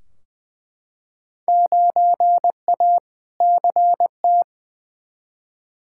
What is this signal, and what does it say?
Morse code, heard as a keyed tone, about 20 wpm, 700 Hz. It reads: 9A CT
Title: Field Day Exchanges Encore - 20wpm